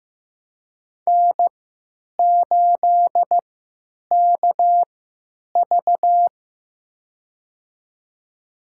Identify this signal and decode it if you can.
Morse code, heard as a keyed tone, about 15 wpm, spaced as for 9 wpm, 700 Hz. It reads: N8KV
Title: USA Call Signs - 4 Characters - 3x Character Spacing - 15wpm